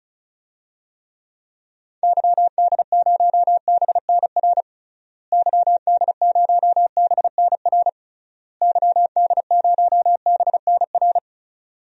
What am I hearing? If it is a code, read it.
Morse code, heard as a keyed tone, about 34 wpm, 700 Hz. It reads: YD0BNR YD0BNR YD0BNR
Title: DXCC-100 Call Signs - 6 Characters - Repeated 3x - 35wpm